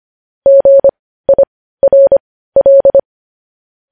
Morse code, heard as a keyed tone, about 25 wpm, spaced as for 15 wpm, 550 Hz. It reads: GIRL
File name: podcast / News headlines-1x-25x15